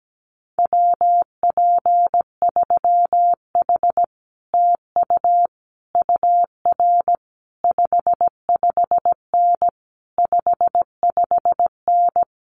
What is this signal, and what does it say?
Morse code, heard as a keyed tone, about 17 wpm, 700 Hz. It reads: WP3H TU UR 55N 55N
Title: Common QSO Phrases Encore 17wpm